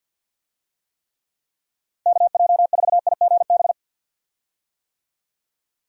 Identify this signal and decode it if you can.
Morse code, heard as a keyed tone, about 50 wpm, 700 Hz. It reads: KJ4IGB
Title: USA Call Signs - 6 Characters - Repeated 3x - 50wpm